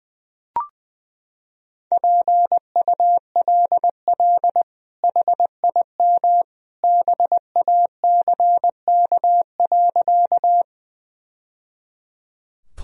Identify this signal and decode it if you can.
Morse code, heard as a keyed tone, about 19 wpm, 700 Hz. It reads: PULL HIM BACK.
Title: Sentences from Top 1000 Words - Review Part 2 - 20wpm